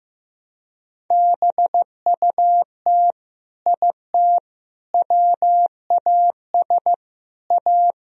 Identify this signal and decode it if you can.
Morse code, heard as a keyed tone, about 15 wpm, 700 Hz. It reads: BUT IT WAS A
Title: Sentences from Top 1000 Words - Review Part 4 - 15wpm